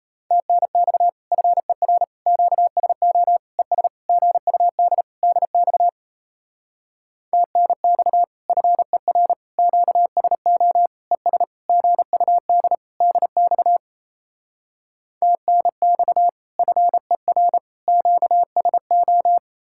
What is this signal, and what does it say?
Morse code, heard as a keyed tone, about 37 wpm, 700 Hz. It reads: TNX FER QSO ES GUD DX TNX FER QSO ES GUD DX TNX FER QSO